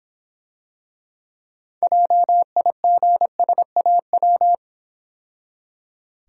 Morse code, heard as a keyed tone, about 26 wpm, 700 Hz. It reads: JIGSAW